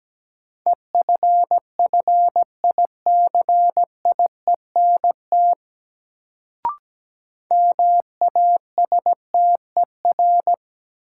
Morse code, heard as a keyed tone, about 17 wpm, 700 Hz. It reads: EFFICIENT MASTER
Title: Sentences from Top 1300 Words - New Word Drill - 17wpm